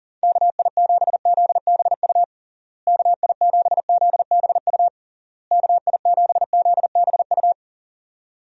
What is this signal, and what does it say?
Morse code, heard as a keyed tone, about 40 wpm, 700 Hz. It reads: KI7ZBU KI7ZBU KI7ZBU
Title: USA Call Signs - 6 Characters - Repeated 3x - 40wpm